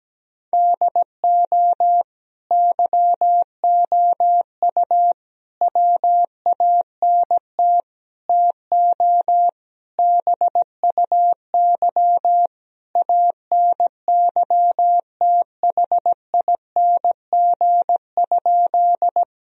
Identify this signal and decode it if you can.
Morse code, heard as a keyed tone, about 17 wpm, 700 Hz. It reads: DO YOU WANT TO BUY ANYTHING?
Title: Sentences from Top 1000 Words - Review Part 4 - 17wpm